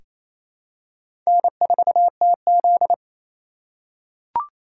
Morse code, heard as a keyed tone, about 28 wpm, 700 Hz. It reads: N4TZ
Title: K1USN SST QSO Practice - 28wpm